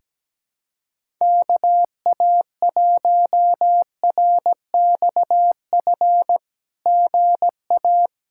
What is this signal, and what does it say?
Morse code, heard as a keyed tone, about 17 wpm, 700 Hz. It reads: KA1RXF GA
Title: SOTA QSO Practice 17wpm